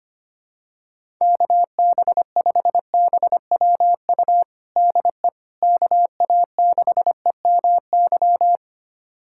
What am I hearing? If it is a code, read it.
Morse code, heard as a keyed tone, about 25 wpm, 700 Hz. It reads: KB5BWU DE KA6EMY
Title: USA Callsign DE Callsign - 6 Characters - 25wpm